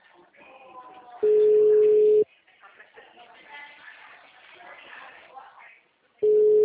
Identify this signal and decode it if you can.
Ringback tone (Europe, 425 Hz), with speech.
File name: out- -2023- - - .629